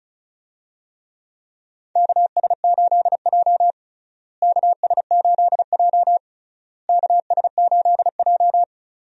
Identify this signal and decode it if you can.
Morse code, heard as a keyed tone, about 34 wpm, 700 Hz. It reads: KS8J KS8J KS8J
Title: USA Call Signs - 4 Characters - Repeated 3x - 35wpm